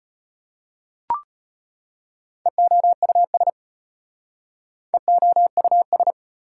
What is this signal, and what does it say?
Morse code, heard as a keyed tone, about 34 wpm, 690 Hz. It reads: EOUS EOUS